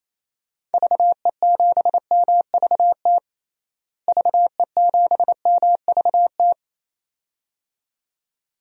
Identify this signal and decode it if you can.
Morse code, heard as a keyed tone, about 28 wpm, 700 Hz. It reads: VE7MVT VE7MVT
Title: DXCC-100 Call Signs - 6 Characters - Repeated 2x - 28wpm